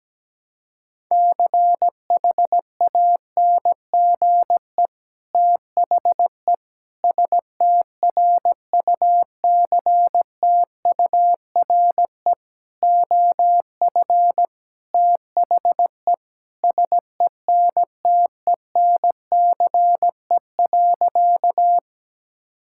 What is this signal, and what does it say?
Morse code, heard as a keyed tone, about 17 wpm, 700 Hz. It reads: CHANGE THE STRUCTURE OF THE SENTENCE.
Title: Sentences from Top 1000 Words - Review Part 6 - 17wpm